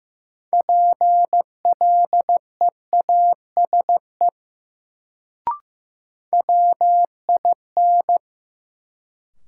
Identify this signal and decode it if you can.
Morse code, heard as a keyed tone, about 15 wpm, 700 Hz. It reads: PLEASE WIN
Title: Sentences from Top 1500 Words - New Word Drill Alt - 15wpm